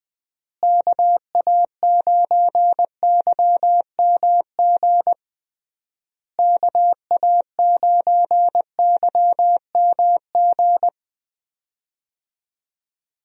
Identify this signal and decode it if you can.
Morse code, heard as a keyed tone, about 21 wpm, 700 Hz. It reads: KA9YMG KA9YMG
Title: USA Call Signs - 6 Characters - Repeated 3x - 20wpm